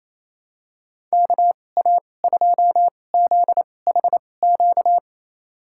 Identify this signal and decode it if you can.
Morse code, heard as a keyed tone, about 28 wpm, spaced as for 20 wpm, 700 Hz. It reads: KA2ZHQ
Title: USA Call Signs - 2x Character Spacing - 28wpm